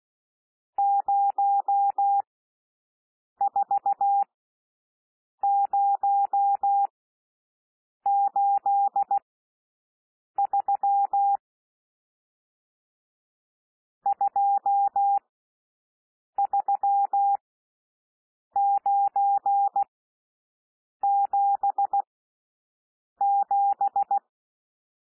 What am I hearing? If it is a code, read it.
Morse code, heard as a keyed tone, about 16 wpm, spaced as for 6 wpm, 800 Hz. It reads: 04083 23977
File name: SOUND NUMBERS 16 wpm